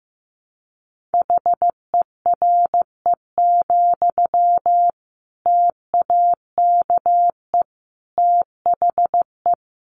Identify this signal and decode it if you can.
Morse code, heard as a keyed tone, about 15 wpm, 700 Hz. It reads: HERE, TAKE THE
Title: Sentences from Top 800 Words 15wpm